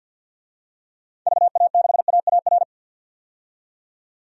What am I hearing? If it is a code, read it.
Morse code, heard as a keyed tone, about 52 wpm, 700 Hz. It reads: VA6AAR
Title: DXCC-100 Call Signs - 6 Characters - 50wpm